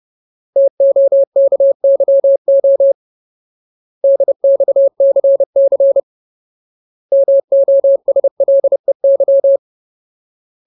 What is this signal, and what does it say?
Morse code, heard as a keyed tone, about 30 wpm, 550 Hz. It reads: TOKYO DXCC MOSLEY
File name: Ultimate Word List / 30WPM Ultimate